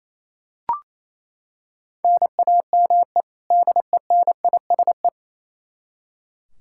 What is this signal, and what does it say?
Morse code, heard as a keyed tone, about 28 wpm, 700 Hz. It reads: NAME DENISE
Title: Practice QSO - Exchange 1 - Beginning - 28wpm